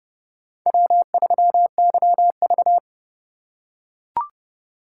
Morse code, heard as a keyed tone, about 30 wpm, 700 Hz. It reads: W3YV